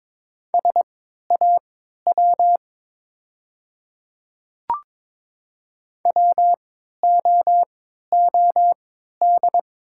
Morse code, heard as a keyed tone, about 22 wpm, spaced as for 12 wpm, 700 Hz. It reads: SAW WOOD
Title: 3x Character Spacing - Top 500 Words - 22wpm